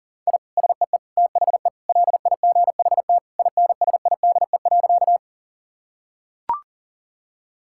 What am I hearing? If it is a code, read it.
Morse code, heard as a keyed tone, about 40 wpm, 700 Hz. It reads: I SEE THE LIGHT INSIDE.
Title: Sentences from Top 1000 Words - Review Part 6 - 40wpm